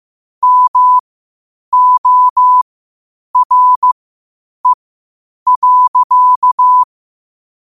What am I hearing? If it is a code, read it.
Morse code, heard as a keyed tone, about 15 wpm, spaced as for 8 wpm, 1000 Hz. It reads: MORE.
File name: qotdfiles / 8wpm